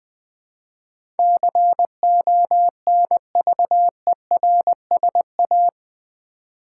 Morse code, heard as a keyed tone, about 20 wpm, 700 Hz. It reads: CONVERSA
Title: Build A Word - 4 Letters At A Time - 12-Letter Words - 20wpm